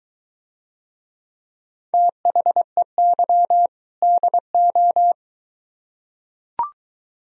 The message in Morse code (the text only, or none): THEY DO